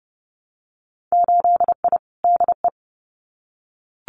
Morse code, heard as a keyed tone, about 30 wpm, 700 Hz. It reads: 8I DE
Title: Winter Field Day Exchanges Encore 30wpm